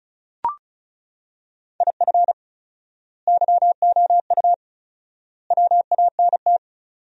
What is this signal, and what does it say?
Morse code, heard as a keyed tone, about 34 wpm, 700 Hz. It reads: IF YOU WANT
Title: 4x Word-Spacing - Sentences from Top 200 Words - 35wpm